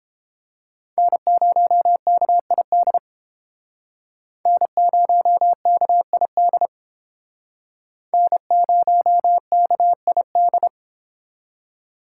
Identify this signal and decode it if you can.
Morse code, heard as a keyed tone, about 30 wpm, 700 Hz. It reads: N0KID N0KID N0KID